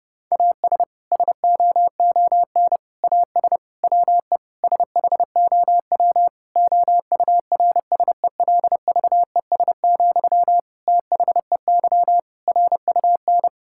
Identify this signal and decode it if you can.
Morse code, heard as a keyed tone, about 32 wpm, 700 Hz. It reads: AS SOON AS WE SHOW OURSELVES, THEY RUN
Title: Sentences from Top 1600 Words 30wpm